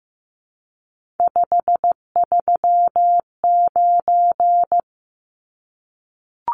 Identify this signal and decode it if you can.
Morse code, heard as a keyed tone, about 15 wpm, 700 Hz. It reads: 539
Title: Numbers - 3-Digit 15wpm